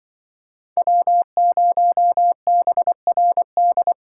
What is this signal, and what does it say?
Morse code, heard as a keyed tone, about 23 wpm, 700 Hz. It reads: W0BRD
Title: Speed Racing - USA 1x3 Call Signs 20wpm